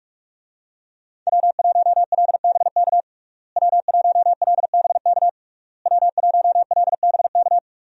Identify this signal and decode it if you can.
Morse code, heard as a keyed tone, about 46 wpm, 700 Hz. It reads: W1LBK W1LBK W1LBK
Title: USA Call Signs - 5 Characters - Repeated 3x - 45wpm